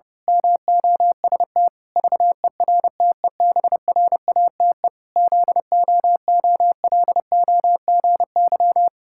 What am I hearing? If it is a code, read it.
Morse code, heard as a keyed tone, about 30 wpm, 700 Hz. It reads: MOST VERTEBRATE ZOOLOGY
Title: Pangram Challenge v2 - 30wpm